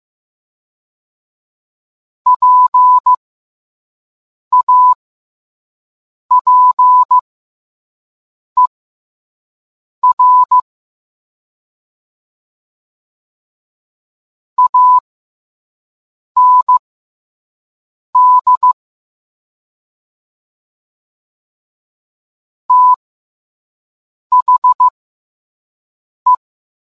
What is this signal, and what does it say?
Morse code, heard as a keyed tone, about 15 wpm, spaced as for 5 wpm, 1000 Hz. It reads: PAPER AND THE